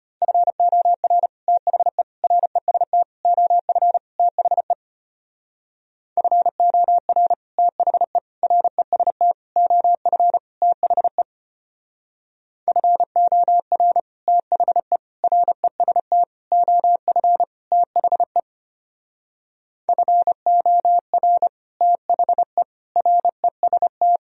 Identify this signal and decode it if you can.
Morse code, heard as a keyed tone, about 30 wpm, 700 Hz. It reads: FOR THE REST OF THE FOR THE REST OF THE FOR THE REST OF THE FOR THE REST
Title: Speed-Racing - Sets of 5 Words 25wpm